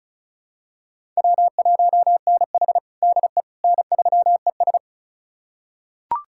Morse code, heard as a keyed tone, about 34 wpm, 700 Hz. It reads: W1NH DE N3ES
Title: USA Callsign DE Callsign - 4 Characters - 35wpm